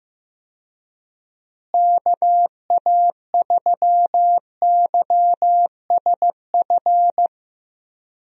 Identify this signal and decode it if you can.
Morse code, heard as a keyed tone, about 15 wpm, 700 Hz. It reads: KA3YSF
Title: USA Call Signs - 6 Characters - Repeated 3x - 15wpm